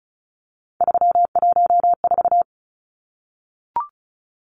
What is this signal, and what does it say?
Morse code, heard as a keyed tone, about 34 wpm, 700 Hz. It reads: 314
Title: Numbers - 3-Digit 35wpm